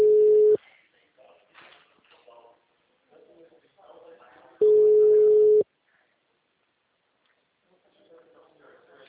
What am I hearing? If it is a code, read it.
Ringback tone (Europe, 425 Hz), with speech.
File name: out- -2005- - - .22216